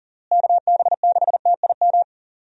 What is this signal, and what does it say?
Morse code, heard as a keyed tone, about 40 wpm, 700 Hz. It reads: KB6TIM
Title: USA Call Signs - 6 Characters - Repeated 3x - 40wpm